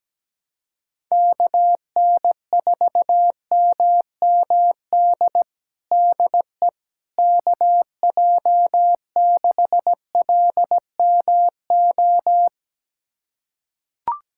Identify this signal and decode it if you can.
Morse code, heard as a keyed tone, about 17 wpm, 700 Hz. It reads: KN4MMD DE KJ6LMO
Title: USA Callsign DE Callsign - 6 Characters - 17wpm